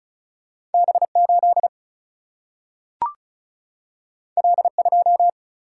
Morse code, heard as a keyed tone, about 36 wpm, 700 Hz. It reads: B8 L2
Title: Single-Letter Single-Number 35wpm